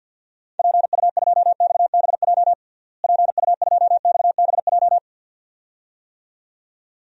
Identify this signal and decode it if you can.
Morse code, heard as a keyed tone, about 46 wpm, 700 Hz. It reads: PU2XBJ PU2XBJ
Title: DXCC-100 Call Signs - 6 Characters - Repeated 2x - 50wpm